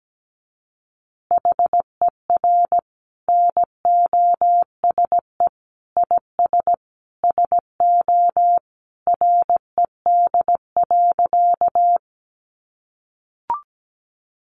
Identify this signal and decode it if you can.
Morse code, heard as a keyed tone, about 17 wpm, 700 Hz. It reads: HER NOSE IS SO RED.
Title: Easy Phrases 17wpm